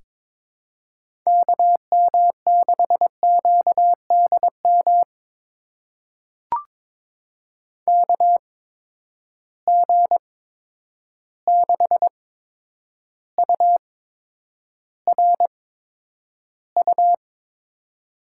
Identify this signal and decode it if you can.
Morse code, heard as a keyed tone, about 22 wpm, spaced as for 9 wpm, 700 Hz. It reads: KM6QDM KG6URU
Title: USA Call Signs - 6 Characters - 8x Character Spacing - 22wpm